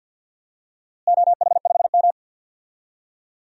Morse code, heard as a keyed tone, about 50 wpm, 700 Hz. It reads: OH5M
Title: DXCC-100 Call Signs - 4 Characters - Repeated 3x - 50wpm